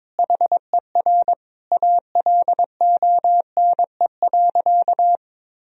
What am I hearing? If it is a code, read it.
Morse code, heard as a keyed tone, about 22 wpm, 700 Hz. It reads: HER ALONE.
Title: Sentences from Top 1000 Words - Review Part 4 - 22wpm